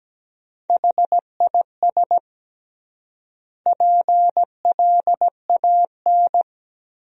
Morse code, heard as a keyed tone, about 17 wpm, 700 Hz. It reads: HIS PLAN
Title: 3x Word-Spacing - Sentences from Top 500 Words - 17wpm